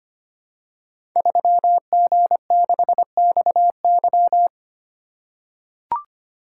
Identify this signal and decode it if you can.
Morse code, heard as a keyed tone, about 25 wpm, 700 Hz. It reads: 3G6XY
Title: Code Groups - 5 Letter:Numbers - 25wpm